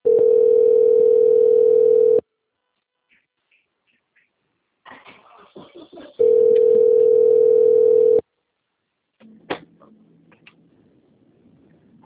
Ringback tone (North America), 440 + 480 Hz.
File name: out- - - - - .93189